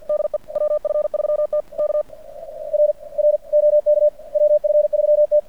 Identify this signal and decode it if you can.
Morse code, heard as a keyed tone, about 48 wpm, 600 Hz. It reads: DE WF3T K R R OK WF3T
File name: archived / CRSnet